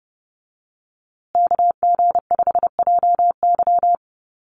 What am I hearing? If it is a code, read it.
Morse code, heard as a keyed tone, about 30 wpm, 700 Hz. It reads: KG5JY
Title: USA Call Signs Repeated 3x 30wpm